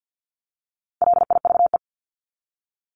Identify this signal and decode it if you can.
Morse code, heard as a keyed tone, about 50 wpm, 700 Hz. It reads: LIVE
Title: Top 500 Words 50wpm